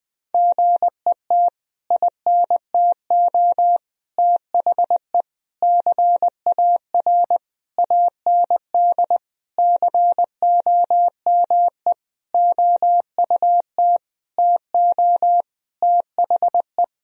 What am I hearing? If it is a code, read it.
Morse code, heard as a keyed tone, about 19 wpm, 700 Hz. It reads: GET INTO THE CAR AND COME OUT TO THE
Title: Sentences from Top 1000 Words - Review Part 1 - 20wpm